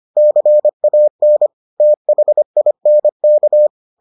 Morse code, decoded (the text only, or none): CAN THINK